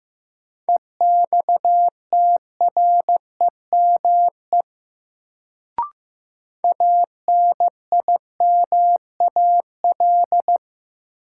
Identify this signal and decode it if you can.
Morse code, heard as a keyed tone, about 15 wpm, 700 Hz. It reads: EXTREME ANIMAL
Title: Sentences from Top 1300 Words - New Word Drill Alt - 15wpm